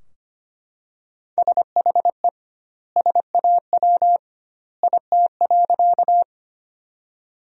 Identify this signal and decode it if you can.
Morse code, heard as a keyed tone, about 25 wpm, 700 Hz. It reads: SHE SAW IT.
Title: 2x Word-Spacing - Sentences from Top 300 Words - 25wpm